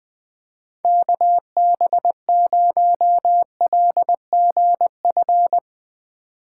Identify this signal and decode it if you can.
Morse code, heard as a keyed tone, about 20 wpm, 700 Hz. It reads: KB0LGF
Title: K1USN SST QSO Practice - 20wpm